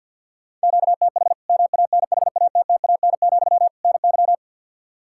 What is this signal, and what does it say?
Morse code, heard as a keyed tone, about 52 wpm, 700 Hz. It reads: QTH MANHATTAN, NY
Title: Practice QSO - Exchange 1 - Beginning - 50wpm